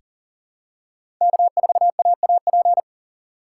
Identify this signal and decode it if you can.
Morse code, heard as a keyed tone, about 40 wpm, 700 Hz. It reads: K4AAP
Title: USA Call Signs - 5 Characters - 6x Character Spacing - 40wpm